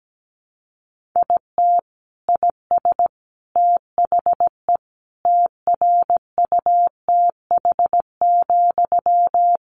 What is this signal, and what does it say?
Morse code, heard as a keyed tone, about 17 wpm, 700 Hz. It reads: IT IS THE TRUTH,
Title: Sentences from Top 800 Words 17wpm